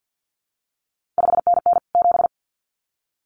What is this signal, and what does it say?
Morse code, heard as a keyed tone, about 50 wpm, 700 Hz. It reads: 5NN 7
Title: CQ WW DX Contest Exchange - 50wpm